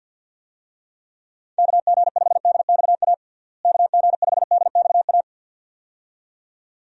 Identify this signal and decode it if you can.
Morse code, heard as a keyed tone, about 52 wpm, 700 Hz. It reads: KG5DXA KG5DXA
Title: USA Call Signs - 6 Characters - Repeated 2x - 50wpm